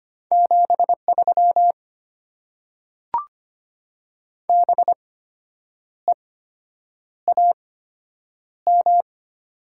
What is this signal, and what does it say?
Morse code, heard as a keyed tone, about 25 wpm, spaced as for 8 wpm, 700 Hz. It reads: 73 BEAM
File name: Top 90 QSO Elements - 8x Character Spacing - 25wpm